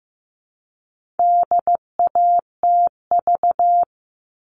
Morse code, heard as a keyed tone, about 15 wpm, 700 Hz. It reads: DATV
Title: All Cut Numbers - 4-Digit 15wpm